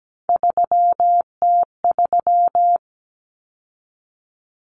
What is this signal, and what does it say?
Morse code, heard as a keyed tone, about 17 wpm, 700 Hz. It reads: 3T3
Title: Common Cut Numbers - 3-Digit 17wpm